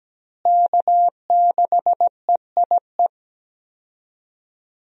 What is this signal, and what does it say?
Morse code, heard as a keyed tone, about 17 wpm, 700 Hz. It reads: K6EIE
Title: USA Call Signs - 5 Characters - Repeated 3x - 17wpm